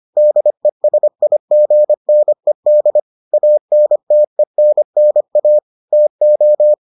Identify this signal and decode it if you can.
Morse code, heard as a keyed tone, about 25 wpm, 600 Hz. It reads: DESIGNED ANTENNA TO